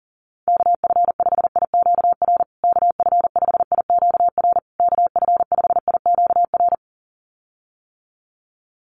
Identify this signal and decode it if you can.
Morse code, heard as a keyed tone, about 40 wpm, 700 Hz. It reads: KF5IQR KF5IQR KF5IQR
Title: USA Call Signs Repeated 3x 40wpm